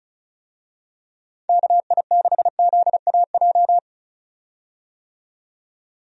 Morse code, heard as a keyed tone, about 36 wpm, 700 Hz. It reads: KI6ZAJ
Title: USA Call Signs - 6 Characters - 35wpm